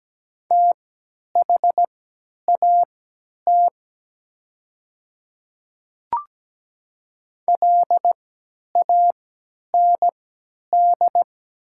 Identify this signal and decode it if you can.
Morse code, heard as a keyed tone, about 17 wpm, spaced as for 10 wpm, 700 Hz. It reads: THAT LAND
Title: 3x Character Spacing - Top 200 Words - 17wpm